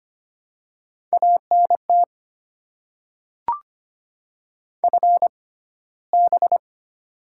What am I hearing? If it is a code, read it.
Morse code, heard as a keyed tone, about 25 wpm, spaced as for 15 wpm, 700 Hz. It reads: ANT FB
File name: Top 20 QSO Elements - 6x Character Spacing - 25wpm